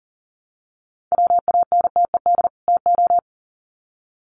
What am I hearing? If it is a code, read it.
Morse code, heard as a keyed tone, about 40 wpm, 700 Hz. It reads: WANTED TO
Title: Sets of 2 Words 40wpm